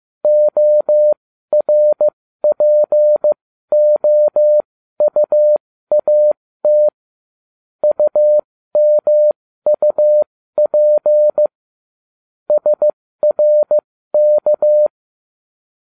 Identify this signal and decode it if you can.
Morse code, heard as a keyed tone, about 15 wpm, spaced as for 13 wpm, 600 Hz. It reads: ORPOUAT UMUP SRK